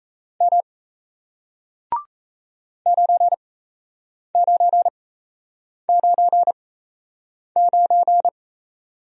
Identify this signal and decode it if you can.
Morse code, heard as a keyed tone, about 38 wpm, spaced as for 8 wpm, 700 Hz. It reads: M 9999